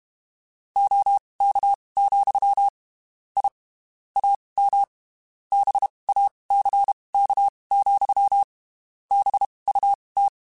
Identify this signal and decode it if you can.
Morse code, heard as a keyed tone, about 32 wpm, spaced as for 23 wpm, 800 Hz. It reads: OK, I AM BACK, BUT
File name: (spaced) / QSOs Spaced-mp3s / WA7QOX 2 @30 sp